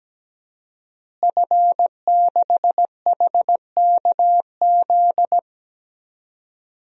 Morse code, heard as a keyed tone, about 17 wpm, 700 Hz. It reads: F6HKZ
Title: DXCC-100 Call Signs - 5 Characters - Repeated 2x - 17wpm